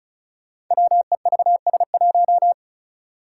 Morse code, heard as a keyed tone, about 36 wpm, 700 Hz. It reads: WEVS1
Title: Five Challenging Letter-Number Combinations 35wpm